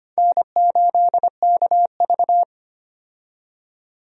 Morse code, heard as a keyed tone, about 25 wpm, 700 Hz. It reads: N8KV
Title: USA Call Signs - 4 Characters - 25wpm